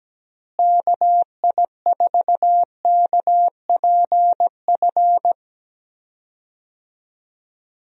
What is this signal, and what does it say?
Morse code, heard as a keyed tone, about 17 wpm, 700 Hz. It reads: KI4KPF
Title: USA Call Signs - 6 Characters - Repeated 3x - 17wpm